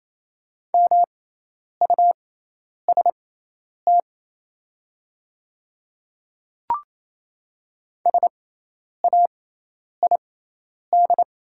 Morse code, heard as a keyed tone, about 28 wpm, spaced as for 10 wpm, 690 Hz. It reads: MUST SAID
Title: 6x Character Spacing - Top 300 Words - 28wpm